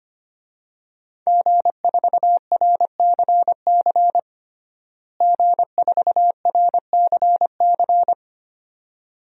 Morse code, heard as a keyed tone, about 25 wpm, 700 Hz. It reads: G4RCC G4RCC
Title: DXCC-100 Call Signs - 5 Characters - Repeated 2x - 25wpm